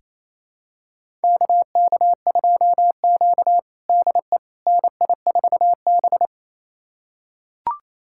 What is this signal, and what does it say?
Morse code, heard as a keyed tone, about 28 wpm, 700 Hz. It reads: KK2Q DE NI4B
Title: USA Callsign DE Callsign - 4 Characters - 28wpm